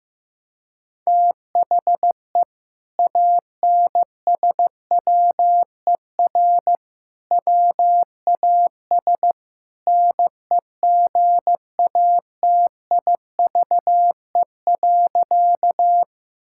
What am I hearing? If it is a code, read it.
Morse code, heard as a keyed tone, about 15 wpm, 700 Hz. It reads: THE ANSWER WAS NEGATIVE.
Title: Sentences from Top 1000 Words - Review Part 6 - 15wpm